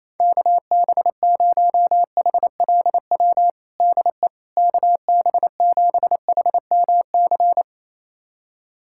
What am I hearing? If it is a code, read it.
Morse code, heard as a keyed tone, about 27 wpm, 700 Hz. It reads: KB0HLW DE KB7HMC
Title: USA Callsign DE Callsign - 6 Characters - 28wpm